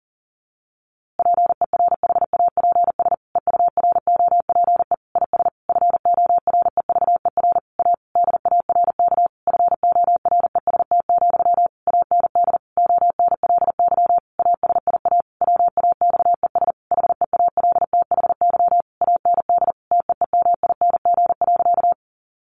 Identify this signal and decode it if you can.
Morse code, heard as a keyed tone, about 37 wpm, 700 Hz. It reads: PERHAPS EUROPE IS FOREVER A DARK FOREST, AND ONLY ASIA WAXES HEALTHY AND TEEMING.
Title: A Sound Of Thunder 40wpm